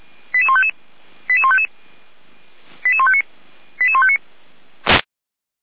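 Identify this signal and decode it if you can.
Five-tone selective call (ZVEI), heard as quick runs of five tones, tones of about 70 ms. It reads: 88155 88155 88159 88159
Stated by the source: ZVEI five-tone selective call received by SDR.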